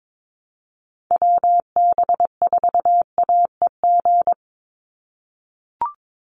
Morse code, heard as a keyed tone, about 22 wpm, 700 Hz. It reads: WB4AEG
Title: Call Signs - 22wpm